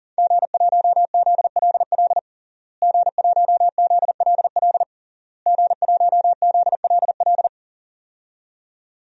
Morse code, heard as a keyed tone, about 40 wpm, 700 Hz. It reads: G1ZLL G1ZLL G1ZLL
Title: DXCC-100 Call Signs - 5 Characters - Repeated 3x - 40wpm